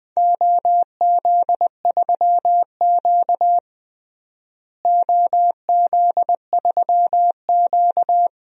Morse code, heard as a keyed tone, about 20 wpm, 700 Hz. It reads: OZ3Q OZ3Q
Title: DXCC-100 Call Signs - 4 Characters - Repeated 3x - 20wpm